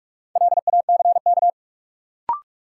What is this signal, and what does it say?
Morse code, heard as a keyed tone, about 46 wpm, 700 Hz. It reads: LACK
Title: Sentences from Top 1100 Words - New Word Drill - 45wpm